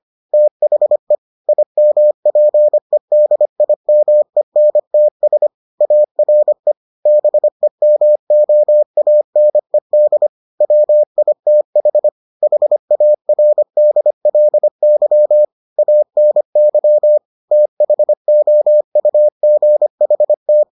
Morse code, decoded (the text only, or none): THE IMPEDIMENTS ARE BEMOANED WITH HARDLY ANY THOUGHT